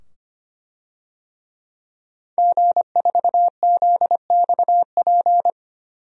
Morse code, heard as a keyed tone, about 25 wpm, 700 Hz. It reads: G4ZXP
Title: DXCC-100 Call Signs - 5 Characters - 25wpm